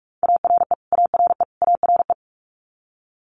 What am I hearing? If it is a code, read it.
Morse code, heard as a keyed tone, about 46 wpm, 690 Hz. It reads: ARE ARE ARE
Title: Multi-letter Phonemes Repeated 3x 45wpm